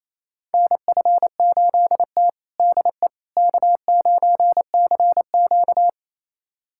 Morse code, heard as a keyed tone, about 28 wpm, 700 Hz. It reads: NF8T DE K9CQ
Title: USA Callsign DE Callsign - 4 Characters - 28wpm